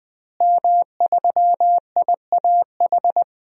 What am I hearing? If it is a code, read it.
Morse code, heard as a keyed tone, about 20 wpm, 700 Hz. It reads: M3IAH
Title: DXCC-100 Call Signs - 5 Characters - Repeated 2x - 20wpm